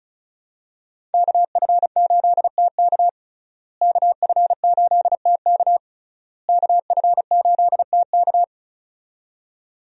Morse code, heard as a keyed tone, about 34 wpm, 700 Hz. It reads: KF8TK KF8TK KF8TK
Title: USA Call Signs - 5 Characters - Repeated 3x - 35wpm